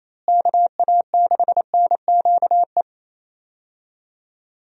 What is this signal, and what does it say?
Morse code, heard as a keyed tone, about 28 wpm, 700 Hz. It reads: KA6NQE
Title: USA Call Signs - 6 Characters - Repeated 3x - 28wpm